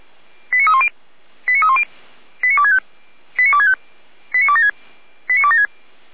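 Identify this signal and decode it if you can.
Five-tone selective call (ZVEI), heard as quick runs of five tones, tones of about 70 ms. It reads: 89310 89310 89265 89265 89276 89276
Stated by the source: ZVEI five-tone selective call received on an SDR, hosted by feuerwehr-orlamuende (Orlamünde fire brigade).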